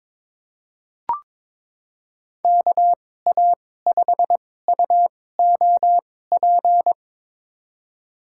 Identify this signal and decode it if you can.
Morse code, heard as a keyed tone, about 22 wpm, spaced as for 16 wpm, 700 Hz. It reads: KA5UOP
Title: USA Call Signs - 6 Characters - 2x Character Spacing - 22wpm